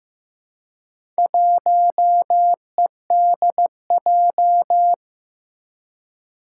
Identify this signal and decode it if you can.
Morse code, heard as a keyed tone, about 15 wpm, 700 Hz. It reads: 1EDJ
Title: Four Challenging Letter-Number Combinations 15wpm